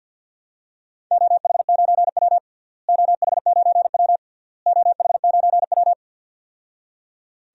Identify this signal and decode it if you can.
Morse code, heard as a keyed tone, about 49 wpm, 700 Hz. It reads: OH9W OH9W OH9W
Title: DXCC-100 Call Signs - 4 Characters - Repeated 3x - 50wpm